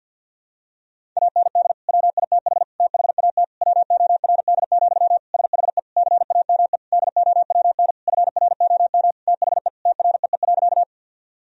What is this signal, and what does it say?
Morse code, heard as a keyed tone, about 51 wpm, 700 Hz. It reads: AND WITH THAT WORD, SHE CAME DOWN FROM THE TREE.
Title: Sentences from Top 1000 Words - Review Part 1 - 50wpm